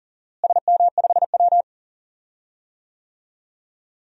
Morse code, heard as a keyed tone, about 40 wpm, 700 Hz. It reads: SM5W
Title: DXCC-100 Call Signs - 4 Characters - Repeated 3x - 40wpm